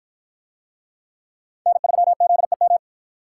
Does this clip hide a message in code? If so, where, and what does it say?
Yes: Morse code from 1.7 to 2.8 s, heard as a keyed tone, about 45 wpm, 700 Hz. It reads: N3ZEM